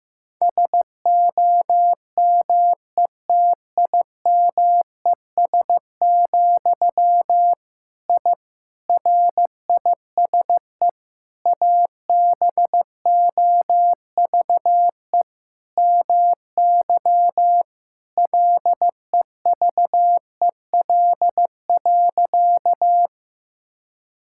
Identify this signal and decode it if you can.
Morse code, heard as a keyed tone, about 15 wpm, 700 Hz. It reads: SOMETIMES, I RISE ABOVE MY LEVEL.
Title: Sentences from Top 1000 Words - Review Part 6 - 15wpm